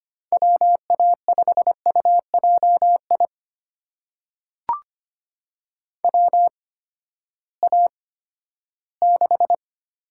Morse code, heard as a keyed tone, about 25 wpm, spaced as for 14 wpm, 700 Hz. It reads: WA5UJI WA6
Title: USA Call Signs - 6 Characters - 8x Character Spacing - 25wpm